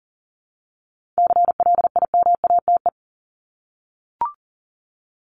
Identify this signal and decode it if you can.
Morse code, heard as a keyed tone, about 40 wpm, 700 Hz. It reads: CLIMATE
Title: Top 1501-1750 Words 40wpm